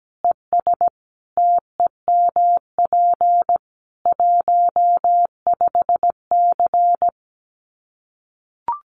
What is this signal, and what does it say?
Morse code, heard as a keyed tone, about 17 wpm, 700 Hz. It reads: ES TEMP 15C
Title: Common QSO Phrases 17wpm